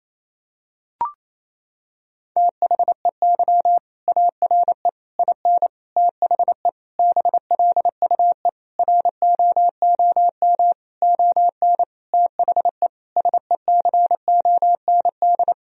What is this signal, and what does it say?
Morse code, heard as a keyed tone, about 28 wpm, 700 Hz. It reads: THEY ARE IN THE BLUE ROOM ON THE SECOND
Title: Sentences from Top 1000 Words - Review Part 3 - 28wpm